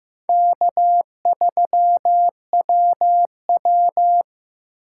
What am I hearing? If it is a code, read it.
Morse code, heard as a keyed tone, about 15 wpm, 700 Hz. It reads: K3WW
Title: USA Call Signs - 4 Characters - 8x Character Spacing - 15wpm